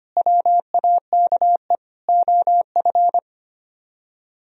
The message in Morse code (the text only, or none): WAKE OF